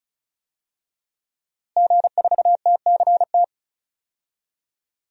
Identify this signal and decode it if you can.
Morse code, heard as a keyed tone, about 36 wpm, 700 Hz. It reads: G4TCT
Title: DXCC-100 Call Signs - 5 Characters - Repeated 2x - 35wpm